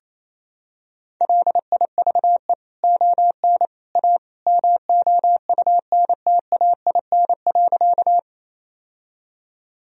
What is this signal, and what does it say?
Morse code, heard as a keyed tone, about 28 wpm, 700 Hz. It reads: LIVE ON A MOUNTAIN.
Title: Sentences from Top 1000 Words - Review Part 1 - 28wpm